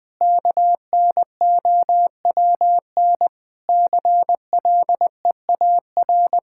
Morse code, heard as a keyed tone, about 20 wpm, 700 Hz. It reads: KNOWN CLEAR
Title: Sentences from Top 1000 Words - Review Part 5 - 20wpm